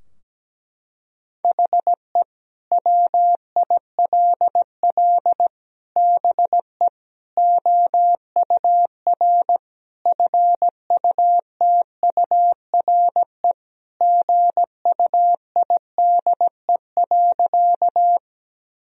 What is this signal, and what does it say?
Morse code, heard as a keyed tone, about 17 wpm, 700 Hz. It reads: HE WILL BE OUR FUTURE GUIDE.
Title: Sentences from Top 1000 Words - Review Part 3 - 17wpm